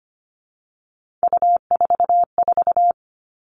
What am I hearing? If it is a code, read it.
Morse code, heard as a keyed tone, about 25 wpm, 700 Hz. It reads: U44
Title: All Cut Numbers - 3-Digit 25wpm